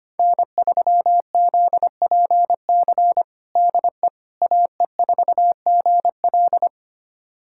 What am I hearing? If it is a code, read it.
Morse code, heard as a keyed tone, about 25 wpm, 700 Hz. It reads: N3ZPC DE AE4GL
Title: USA Callsign DE Callsign - 5 Characters - 25wpm